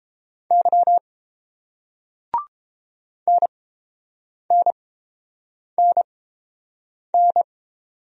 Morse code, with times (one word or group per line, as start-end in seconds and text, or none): 0.5-1.0 Y
3.3-7.4 NNNN